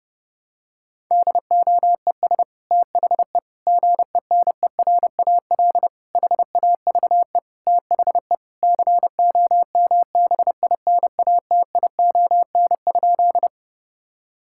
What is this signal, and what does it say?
Morse code, heard as a keyed tone, about 30 wpm, 700 Hz. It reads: DOES THE GENERAL HAVE THE COMBINATION?
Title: Sentences from Top 1000 Words - Review Part 6 - 30wpm